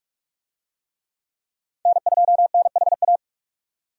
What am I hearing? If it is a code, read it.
Morse code, heard as a keyed tone, about 46 wpm, 700 Hz. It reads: N2NHA
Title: USA Call Signs - 5 Characters - 45wpm